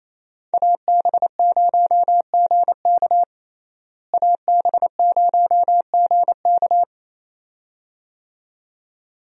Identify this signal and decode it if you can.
Morse code, heard as a keyed tone, about 28 wpm, 700 Hz. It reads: AB0GK AB0GK
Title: USA Call Signs - 5 Characters - Repeated 2x - 28wpm